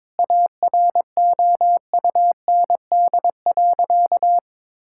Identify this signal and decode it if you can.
Morse code, heard as a keyed tone, about 22 wpm, 700 Hz. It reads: AROUND.
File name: practice / 22wpm / Sentences with 2-Word Phrases - 22wpm